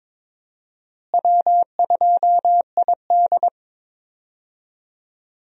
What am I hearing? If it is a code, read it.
Morse code, heard as a keyed tone, about 22 wpm, 700 Hz. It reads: W2ID
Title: USA Call Signs - 4 Characters - Repeated 3x - 22wpm